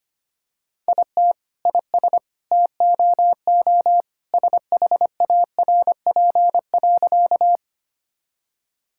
Morse code, heard as a keyed tone, about 25 wpm, 700 Hz. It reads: IT IS TOO SHARP.
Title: Sentences from Top 1600 Words 25wpm